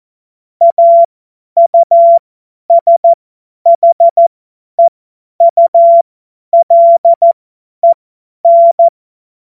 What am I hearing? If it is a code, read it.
Morse code, heard as a keyed tone, about 14 wpm, spaced as for 10 wpm, 680 Hz. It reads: AUSHEULEN
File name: Morsekurs / Witze / Witz 23. Dezember 2025 50 F70